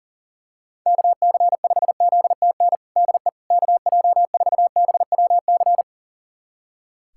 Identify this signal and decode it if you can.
Morse code, heard as a keyed tone, about 40 wpm, 700 Hz. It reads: KC5ZTN DE KJ4BWC
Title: USA Callsign DE Callsign 40wpm